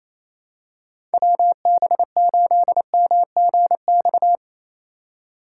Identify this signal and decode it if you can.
Morse code, heard as a keyed tone, about 28 wpm, 700 Hz. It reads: WB8MGX
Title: USA Call Signs - 6 Characters - 3x Character Spacing - 28wpm